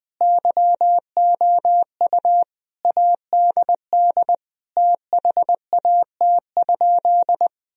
Morse code, heard as a keyed tone, about 20 wpm, 700 Hz. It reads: YOU ADD THAT?
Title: Sentences from Top 1000 Words - Review Part 1 - 20wpm